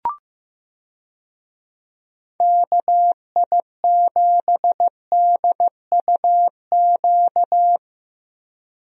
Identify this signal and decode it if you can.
Morse code, heard as a keyed tone, about 15 wpm, 700 Hz. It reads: KI7DUQ
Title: USA Call Signs - 6 Characters - Repeated 3x - 15wpm